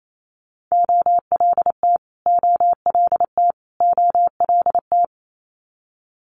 Morse code, heard as a keyed tone, about 28 wpm, 700 Hz. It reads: OLT OLT OLT
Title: Multi-letter Phonemes Repeated 3x 28wpm